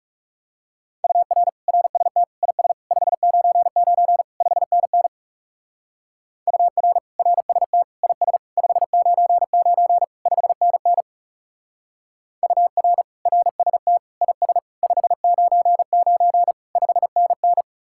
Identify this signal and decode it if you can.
Morse code, heard as a keyed tone, about 34 wpm, 700 Hz. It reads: UR RST IS 599 5NN UR RST IS 599 5NN UR RST IS 599 5NN
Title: Speed-Racing Common QSO Phrases - 30wpm